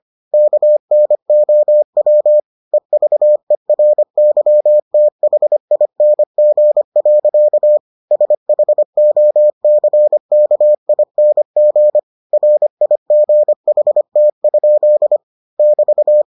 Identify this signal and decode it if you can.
Morse code, heard as a keyed tone, about 25 wpm, 600 Hz. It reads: KNOW EVERYTHING. SHOCKING RIGHT? <BT>